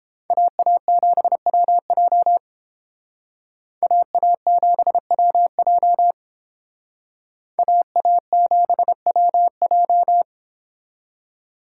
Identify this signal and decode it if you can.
Morse code, heard as a keyed tone, about 30 wpm, 700 Hz. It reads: AA7WJ AA7WJ AA7WJ